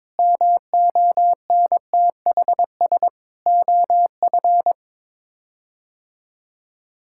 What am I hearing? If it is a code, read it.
Morse code, heard as a keyed tone, about 21 wpm, 700 Hz. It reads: MONTHS OF
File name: practice / 22wpm / Sets of 6 Words - 22wpm